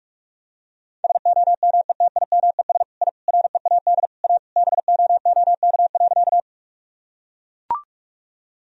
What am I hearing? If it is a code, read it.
Morse code, heard as a keyed tone, about 41 wpm, 700 Hz. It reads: SOMETIMES I READ A BOOK.
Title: Sentences from Top 1000 Words - Review Part 3 - 45wpm